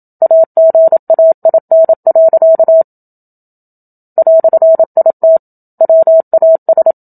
Morse code, heard as a keyed tone, about 27 wpm, 650 Hz. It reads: AGAIN. "IT WAS
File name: Book Two / 01-Chapter 1 - 27 WPM